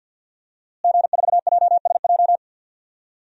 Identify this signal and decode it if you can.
Morse code, heard as a keyed tone, about 49 wpm, 700 Hz. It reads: G4JSJ
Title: DXCC-100 Call Signs - 5 Characters - Repeated 2x - 50wpm